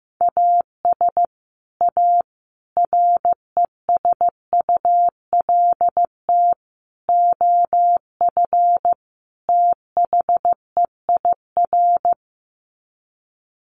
Morse code, heard as a keyed tone, about 15 wpm, 700 Hz. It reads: AS A RESULT OF THEIR
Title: Sets of 5 Words Encore2 15wpm